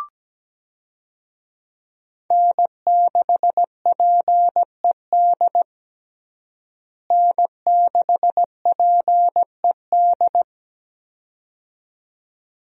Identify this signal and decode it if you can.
Morse code, heard as a keyed tone, about 17 wpm, 700 Hz. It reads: N6PED N6PED
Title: USA Call Signs - 5 Characters - Repeated 2x - 17wpm